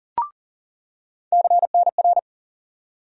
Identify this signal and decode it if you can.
Morse code, heard as a keyed tone, about 40 wpm, 700 Hz. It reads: CNR
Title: USA Call Sign Postfixes - 3 Letters 40wpm